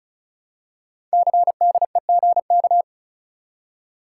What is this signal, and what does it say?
Morse code, heard as a keyed tone, about 36 wpm, 700 Hz. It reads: CDEGK
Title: Code Groups - 5 Letters - 35wpm